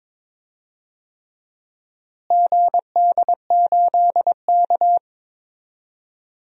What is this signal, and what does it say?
Morse code, heard as a keyed tone, about 22 wpm, 700 Hz. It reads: GD8K
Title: DXCC-100 Call Signs - 4 Characters - Repeated 3x - 22wpm